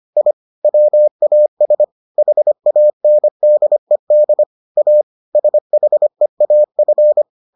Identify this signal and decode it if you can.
Morse code, heard as a keyed tone, about 25 wpm, 600 Hz. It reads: I WAS HANDED A SHEAF